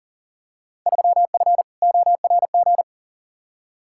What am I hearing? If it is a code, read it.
Morse code, heard as a keyed tone, about 37 wpm, 700 Hz. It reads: 3F ORG
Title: Field Day Exchanges - 40wpm